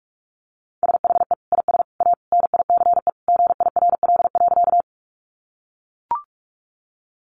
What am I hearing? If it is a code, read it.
Morse code, heard as a keyed tone, about 44 wpm, 700 Hz. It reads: SHE IS A NICE GIRL.
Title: Sentences from Top 1100 Words 45wpm